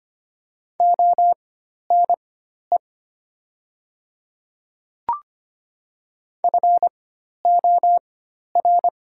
Morse code, heard as a keyed tone, about 25 wpm, spaced as for 12 wpm, 690 Hz. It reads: ONE FOR
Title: 4x Character Spacing - Top 100 Words - 25wpm